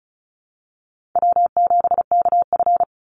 Morse code, heard as a keyed tone, about 34 wpm, 700 Hz. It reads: W7KF
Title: Easy Call Signs Repeated 2x - 35wpm